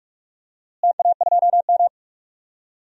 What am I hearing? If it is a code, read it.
Morse code, heard as a keyed tone, about 42 wpm, 700 Hz. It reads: TA2M
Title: DXCC-100 Call Signs - 4 Characters - Repeated 3x - 45wpm